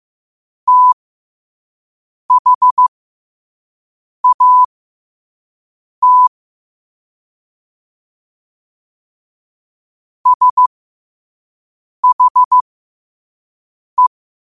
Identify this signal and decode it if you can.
Morse code, heard as a keyed tone, about 15 wpm, spaced as for 5 wpm, 1000 Hz. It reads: THAT SHE
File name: qotdfiles / 5wpm